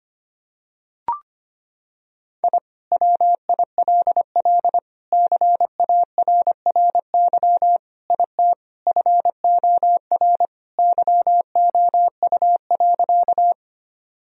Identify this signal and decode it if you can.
Morse code, heard as a keyed tone, about 25 wpm, 700 Hz. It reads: I WILL CARRY IT FOR YOU.
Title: Sentences from Top 1000 Words - Review Part 1 - 25wpm